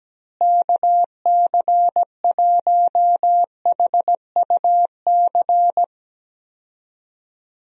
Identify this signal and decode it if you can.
Morse code, heard as a keyed tone, about 17 wpm, 700 Hz. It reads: KC1HUC
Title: USA Call Signs - 6 Characters - Repeated 2x - 17wpm